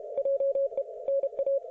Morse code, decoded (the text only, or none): JE NA